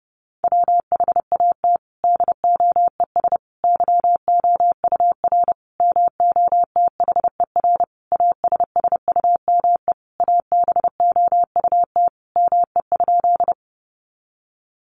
Morse code, heard as a keyed tone, about 30 wpm, 700 Hz. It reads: WHAT DOES YOUR MOTHER ASSUME ABOUT ME?
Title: Sentences from Top 1500 Words 30wpm